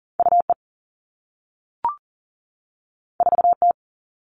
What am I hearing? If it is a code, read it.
Morse code, heard as a keyed tone, about 38 wpm, 700 Hz. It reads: UE 4T
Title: All Cut Numbers - 2-Digit 40wpm